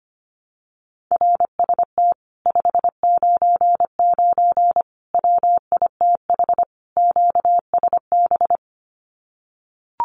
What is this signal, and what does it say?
Morse code, heard as a keyed tone, about 25 wpm, 700 Hz. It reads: RST 599 WITH QSB
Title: Common QSO Phrases Encore 25wpm